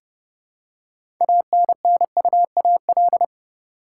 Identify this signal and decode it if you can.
Morse code, heard as a keyed tone, about 32 wpm, 700 Hz. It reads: ANNUAL
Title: Six Letter Words - 30wpm